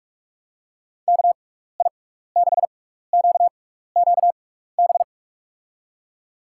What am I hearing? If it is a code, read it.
Morse code, heard as a keyed tone, about 44 wpm, spaced as for 15 wpm, 700 Hz. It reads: KI6QQB
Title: USA Call Signs - 6 Characters - 6x Character Spacing - 45wpm